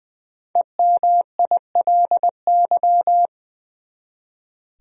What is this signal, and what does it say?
Morse code, heard as a keyed tone, about 21 wpm, 700 Hz. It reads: EMILY
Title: USA Common Names - 20wpm